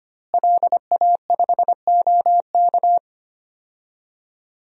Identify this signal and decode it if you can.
Morse code, heard as a keyed tone, about 25 wpm, 700 Hz. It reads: LA5OK
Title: DXCC-100 Call Signs - 5 Characters - Repeated 3x - 25wpm